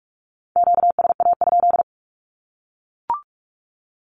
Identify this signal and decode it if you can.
Morse code, heard as a keyed tone, about 44 wpm, 700 Hz. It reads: QSA?
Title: Q-Codes 45wpm